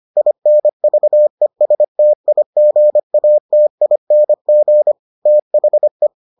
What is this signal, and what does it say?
Morse code, heard as a keyed tone, about 25 wpm, 600 Hz. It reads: INVESTIGATING THE